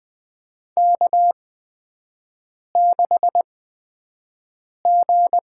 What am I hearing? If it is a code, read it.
Morse code, heard as a keyed tone, about 20 wpm, spaced as for 5 wpm, 700 Hz. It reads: K6G
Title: USA Call Signs - 4 Characters - 8x Character Spacing - 20wpm